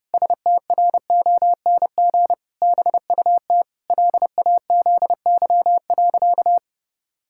Morse code, heard as a keyed tone, about 30 wpm, 700 Hz. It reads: STRONG BUT LAZY.
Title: Pangram Challenge v2 - 30wpm